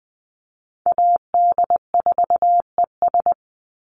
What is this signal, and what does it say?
Morse code, heard as a keyed tone, about 20 wpm, 700 Hz. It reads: AD4ES
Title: Call Signs Repeated 3x - 20wpm